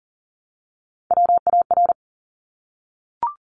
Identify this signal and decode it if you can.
Morse code, heard as a keyed tone, about 40 wpm, 700 Hz. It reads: WAR
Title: Top Three Letter Combinations in Words 40wpm